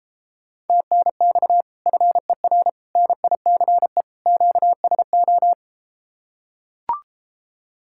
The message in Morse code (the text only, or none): TNX FER NICE QSO